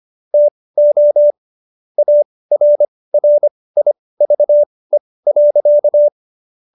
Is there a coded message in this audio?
Yes: Morse code from 0.3 to 6.1 s, heard as a keyed tone, about 25 wpm, spaced as for 18 wpm, 590 Hz. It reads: TO ARRIVE.